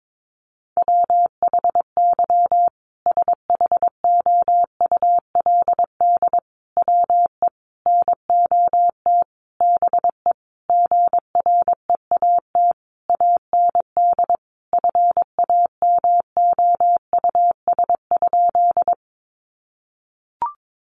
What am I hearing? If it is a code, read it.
Morse code, heard as a keyed tone, about 22 wpm, 700 Hz. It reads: WHY SHOULD WE NOT BE GREAT AND FAMOUS?
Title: Sentences from Top 1500 Words 22wpm